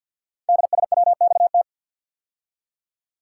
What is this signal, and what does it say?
Morse code, heard as a keyed tone, about 51 wpm, 700 Hz. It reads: DSWXT
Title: Code Groups - 5 Letters - 50wpm